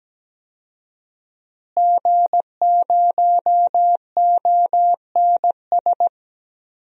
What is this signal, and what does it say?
Morse code, heard as a keyed tone, about 17 wpm, 700 Hz. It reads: G0ONS
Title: DXCC-100 Call Signs - 5 Characters - Repeated 3x - 17wpm